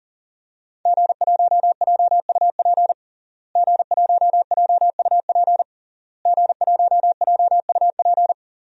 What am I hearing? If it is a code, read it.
Morse code, heard as a keyed tone, about 40 wpm, 700 Hz. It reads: G1JUP G1JUP G1JUP
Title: DXCC-100 Call Signs - 5 Characters - Repeated 3x - 40wpm